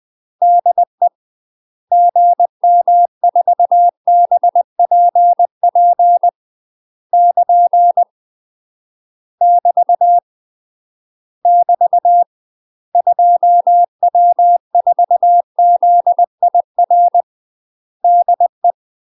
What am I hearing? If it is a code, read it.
Morse code, heard as a keyed tone, about 20 wpm, 700 Hz. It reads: DE GM4BPP <KN> <BT> <BT> 2W4ZIR DE
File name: MASTER QSOs TRAINING/20WPM